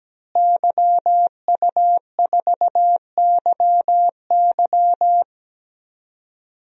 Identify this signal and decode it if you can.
Morse code, heard as a keyed tone, about 17 wpm, 700 Hz. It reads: YU4YY
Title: DXCC-100 Call Signs - 5 Characters - Repeated 3x - 17wpm